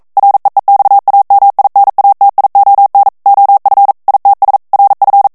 Morse code, heard as a keyed tone, about 42 wpm, 800 Hz. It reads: REEXAMINATION OF ITS RF